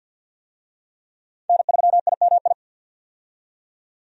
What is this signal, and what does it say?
Morse code, heard as a keyed tone, about 50 wpm, 700 Hz. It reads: N3IMI
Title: USA Call Signs - 5 Characters - 50wpm